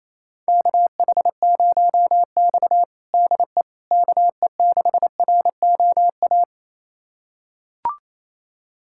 Morse code, heard as a keyed tone, about 28 wpm, 700 Hz. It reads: KH0X DE KE6ROA
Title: USA Callsign DE Callsign 28wpm